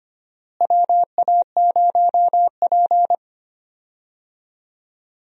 Morse code, heard as a keyed tone, about 25 wpm, 700 Hz. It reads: WA0P
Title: USA Call Signs - 4 Characters - Repeated 2x - 25wpm